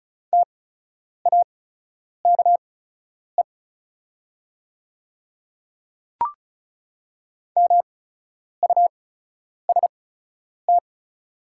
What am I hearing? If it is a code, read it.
Morse code, heard as a keyed tone, about 36 wpm, spaced as for 10 wpm, 690 Hz. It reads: TAKE MUST
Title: 8x Character Spacing - Top 200 Words - 35wpm